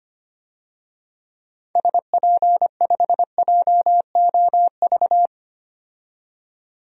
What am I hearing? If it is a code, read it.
Morse code, heard as a keyed tone, about 25 wpm, 700 Hz. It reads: SP5JOV
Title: DXCC-100 Call Signs - 6 Characters - 25wpm